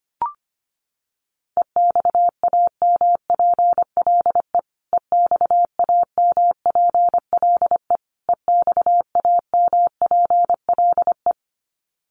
Morse code, heard as a keyed tone, about 25 wpm, 700 Hz. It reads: EXAMPLE EXAMPLE EXAMPLE
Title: Top 500 Words Repeated 25wpm